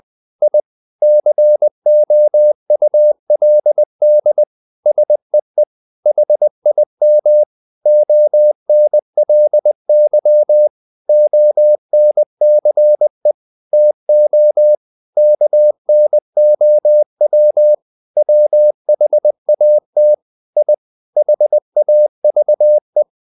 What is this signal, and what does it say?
Morse code, heard as a keyed tone, about 20 wpm, 600 Hz. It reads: I COULD SEE HIM ONLY ONCE TO KNOW WHAT I HAVE
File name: the old man and the sea / the old man and the sea part 1 20wpm